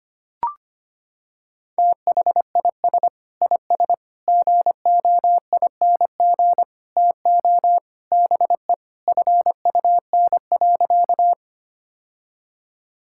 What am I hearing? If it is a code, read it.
Morse code, heard as a keyed tone, about 25 wpm, 700 Hz. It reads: THIS IS GOING TO BE FUN.
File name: practice / 25wpm / Sentences with 4-Word Phrases - 25wpm